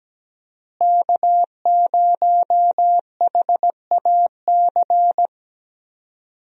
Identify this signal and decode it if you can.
Morse code, heard as a keyed tone, about 17 wpm, 700 Hz. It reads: K0HAC
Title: USA Call Signs - 5 Characters - Repeated 3x - 17wpm